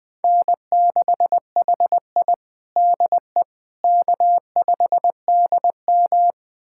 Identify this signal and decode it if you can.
Morse code, heard as a keyed tone, about 20 wpm, 700 Hz. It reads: N6HI DE K5DM
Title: USA Callsign DE Callsign - 4 Characters - 20wpm